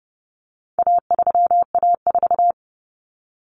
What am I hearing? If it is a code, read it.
Morse code, heard as a keyed tone, about 30 wpm, 700 Hz. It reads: A3A4
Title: Common Cut Numbers - 4-Digit 30wpm